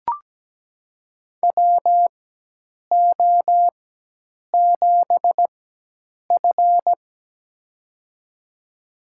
Morse code, heard as a keyed tone, about 17 wpm, spaced as for 8 wpm, 700 Hz. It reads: WO7F
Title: USA Call Signs - 4 Characters - 4x Character Spacing - 17wpm